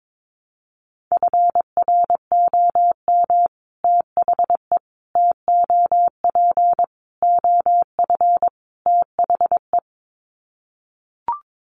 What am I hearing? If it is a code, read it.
Morse code, heard as a keyed tone, about 22 wpm, 700 Hz. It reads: FROM THE TOP OF THE
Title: Sets of 5 Words Encore 22wpm